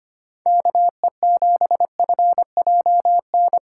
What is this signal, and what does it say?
Morse code, heard as a keyed tone, about 25 wpm, 700 Hz. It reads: KE7FJN
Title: USA Call Signs - 6 Characters - 4x Character Spacing - 25wpm